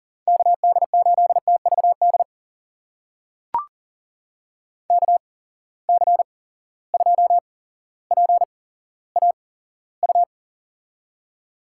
Morse code, heard as a keyed tone, about 38 wpm, spaced as for 17 wpm, 700 Hz. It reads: KD8TVD KC2PAU
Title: USA Call Signs - 6 Characters - 8x Character Spacing - 40wpm